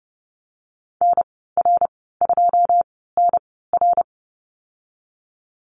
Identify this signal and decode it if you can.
Morse code, heard as a keyed tone, about 30 wpm, spaced as for 17 wpm, 700 Hz. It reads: NR2NR
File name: Lesson 14g - Number 2 - Call Signs 30wpm